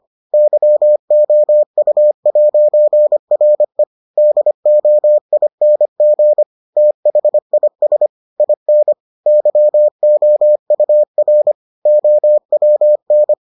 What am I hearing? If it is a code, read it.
Morse code, heard as a keyed tone, about 25 wpm, 600 Hz. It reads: YOU'RE DOING THIS IN YOUR OWN